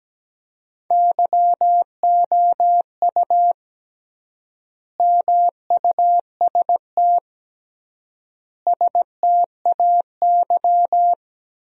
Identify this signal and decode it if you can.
Morse code, heard as a keyed tone, about 17 wpm, 700 Hz. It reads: YOU MUST STAY
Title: 3x Word-Spacing - Sentences from Top 500 Words - 17wpm